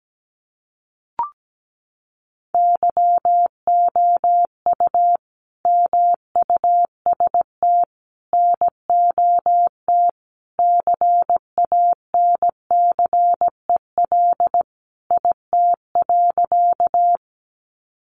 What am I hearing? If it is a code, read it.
Morse code, heard as a keyed tone, about 17 wpm, 700 Hz. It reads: YOU MUST NOT CANCEL IT.
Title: Sentences from Top 1200 Words 17wpm